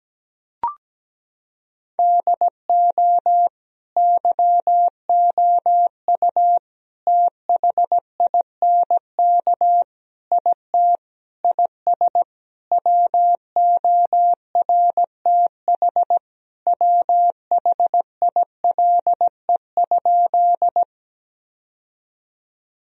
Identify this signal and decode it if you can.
Morse code, heard as a keyed tone, about 17 wpm, 700 Hz. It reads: DO YOU THINK IT IS WORTH WHILE?
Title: Sentences from Top 1000 Words - Review Part 1 - 17wpm